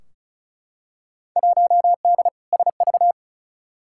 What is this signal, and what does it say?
Morse code, heard as a keyed tone, about 34 wpm, 700 Hz. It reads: 1D SV
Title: Field Day Exchanges - 35wpm